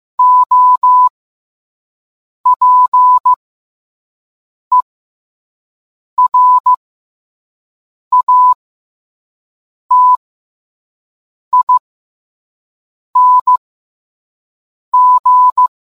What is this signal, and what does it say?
Morse code, heard as a keyed tone, about 15 wpm, spaced as for 5 wpm, 1000 Hz. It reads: OPERATING